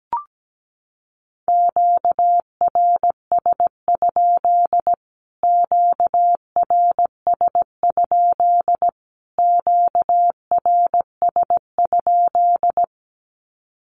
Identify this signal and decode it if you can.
Morse code, heard as a keyed tone, about 17 wpm, 700 Hz. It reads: QRS? QRS? QRS?
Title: Q-Codes Repeated 3x 17wpm